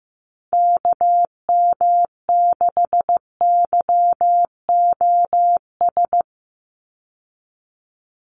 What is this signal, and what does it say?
Morse code, heard as a keyed tone, about 15 wpm, 700 Hz. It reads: KM6YOS
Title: USA Call Signs 15wpm